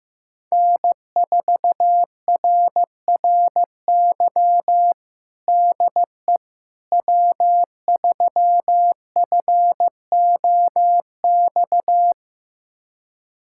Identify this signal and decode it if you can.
Morse code, heard as a keyed tone, about 15 wpm, 700 Hz. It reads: N4RRY DE W3FOX
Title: USA Callsign DE Callsign - 5 Characters - 15wpm